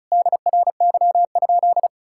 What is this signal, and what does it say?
Morse code, heard as a keyed tone, about 36 wpm, 700 Hz. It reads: DRY?
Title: Sentences from Top 1600 Words 35wpm